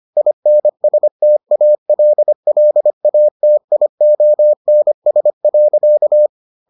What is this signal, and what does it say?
Morse code, heard as a keyed tone, about 25 wpm, 600 Hz. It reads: INSTALLATIONS.